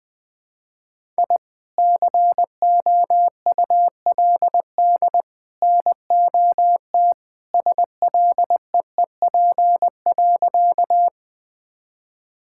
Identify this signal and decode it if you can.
Morse code, heard as a keyed tone, about 21 wpm, 700 Hz. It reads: I COULD NOT SLEEP.
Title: Sentences with 3-Word Phrases - Spoken Beginning - 20wpm